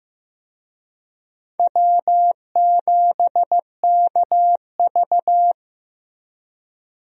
Morse code, heard as a keyed tone, about 15 wpm, 700 Hz. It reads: W7KV
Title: USA Call Signs - 4 Characters - Repeated 2x - 15wpm